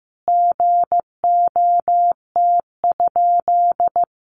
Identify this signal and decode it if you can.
Morse code, heard as a keyed tone, about 15 wpm, 700 Hz. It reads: GOT?
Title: Sentences from Top 300 Words 15wpm